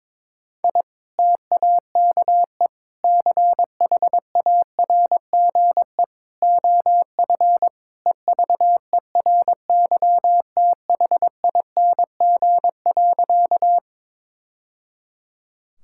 Morse code, heard as a keyed tone, about 22 wpm, 700 Hz. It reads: I TAKE CHARGE OF EVERYTHING.
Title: Sentences from Top 1000 Words - Review Part 6 - 22wpm